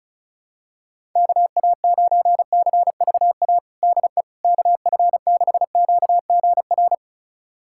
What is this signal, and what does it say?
Morse code, heard as a keyed tone, about 34 wpm, 700 Hz. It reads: KA9CVA DE KF6QGR
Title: USA Callsign DE Callsign 35wpm